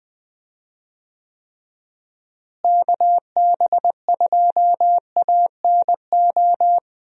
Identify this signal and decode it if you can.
Morse code, heard as a keyed tone, about 20 wpm, 700 Hz. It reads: KB2ANO
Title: USA Call Signs - 6 Characters - Repeated 3x - 20wpm